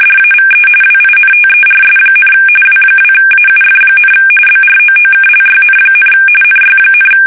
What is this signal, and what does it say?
RTTY, about 45.45 baud. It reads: DAVE WINTERS, W2AUF, BROOKLYN, NY,
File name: fsk850 / teletype-RTTY